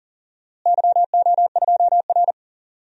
Morse code, heard as a keyed tone, about 40 wpm, 700 Hz. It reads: YO2R
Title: DXCC-100 Call Signs - 4 Characters - Repeated 3x - 40wpm